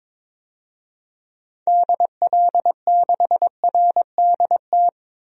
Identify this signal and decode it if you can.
Morse code, heard as a keyed tone, about 21 wpm, 700 Hz. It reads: DL6RDT
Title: DXCC-100 Call Signs - 6 Characters - 22wpm